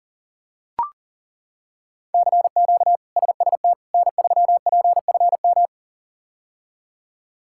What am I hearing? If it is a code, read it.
Morse code, heard as a keyed tone, about 40 wpm, 700 Hz. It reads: CQ SST N3PFM
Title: K1USN SST QSO Practice - 40wpm